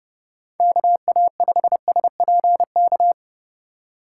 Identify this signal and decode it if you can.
Morse code, heard as a keyed tone, about 30 wpm, 700 Hz. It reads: KA5SPK